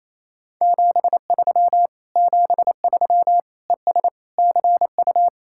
Morse code, heard as a keyed tone, about 34 wpm, 700 Hz. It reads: 73 73 ES CU
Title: Speed-Racing Common QSO Phrases - 28wpm